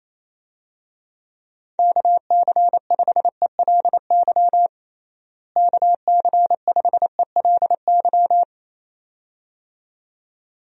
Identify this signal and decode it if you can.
Morse code, heard as a keyed tone, about 28 wpm, 700 Hz. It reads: KC5ELY KC5ELY
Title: USA Call Signs - 6 Characters - Repeated 2x - 28wpm